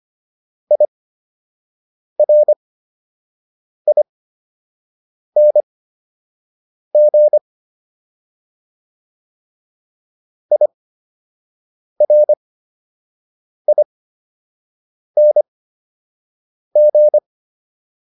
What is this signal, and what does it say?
Morse code, heard as a keyed tone, about 23 wpm, spaced as for 6 wpm, 600 Hz. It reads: IRING IRING